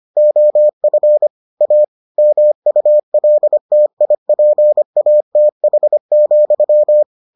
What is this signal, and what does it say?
Morse code, heard as a keyed tone, about 25 wpm, 600 Hz. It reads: OF A MULTIPATH,